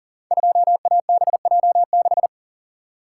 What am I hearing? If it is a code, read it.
Morse code, heard as a keyed tone, about 40 wpm, 700 Hz. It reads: 2ABJ6
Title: Code Groups - 5 Letter:Numbers - 40wpm